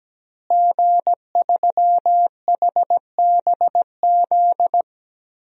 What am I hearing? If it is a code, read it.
Morse code, heard as a keyed tone, about 17 wpm, 700 Hz. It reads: G3HBZ
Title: DXCC-100 Call Signs - 5 Characters - Repeated 2x - 17wpm